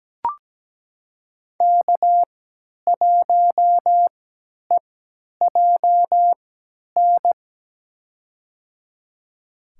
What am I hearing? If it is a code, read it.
Morse code, heard as a keyed tone, about 17 wpm, spaced as for 10 wpm, 700 Hz. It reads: K1EJN
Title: USA Call Signs - 5 Characters - 3x Character Spacing - 17wpm